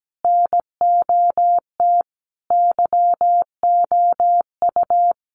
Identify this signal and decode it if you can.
Morse code, heard as a keyed tone, about 17 wpm, 700 Hz. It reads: NOT YOU
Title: Sets of 5 Words Encore 17wpm